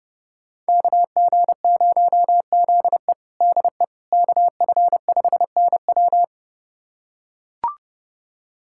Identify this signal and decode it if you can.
Morse code, heard as a keyed tone, about 30 wpm, 700 Hz. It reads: KG0ZE DE KF5NW
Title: USA Callsign DE Callsign - 5 Characters - 30wpm